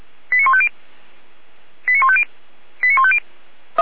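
Five-tone selective call (ZVEI), heard as quick runs of five tones, tones of about 70 ms. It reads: 89150 89150 89150
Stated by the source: ZVEI five-tone selective call received on an SDR.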